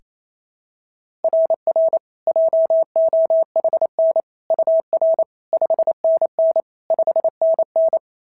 Morse code, heard as a keyed tone, about 27 wpm, 690 Hz. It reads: RR JOHN UR 5NN 5NN
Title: SOTA QSO Practice 28wpm